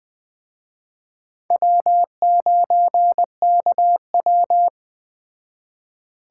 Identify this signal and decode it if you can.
Morse code, heard as a keyed tone, about 20 wpm, 700 Hz. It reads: W9KW
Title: Speed Racing - USA 1x2 Call Signs 17wpm